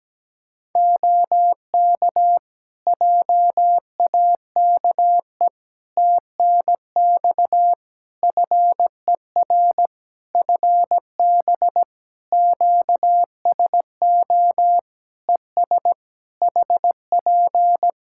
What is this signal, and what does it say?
Morse code, heard as a keyed tone, about 17 wpm, 700 Hz. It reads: OK JAKE TNX FER FB QSO ES HP
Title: Practice QSO - Exchange 4 - Ending - 17wpm